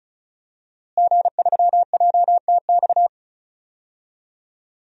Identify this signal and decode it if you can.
Morse code, heard as a keyed tone, about 34 wpm, 700 Hz. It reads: G3JTX
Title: DXCC-100 Call Signs - 5 Characters - Repeated 3x - 35wpm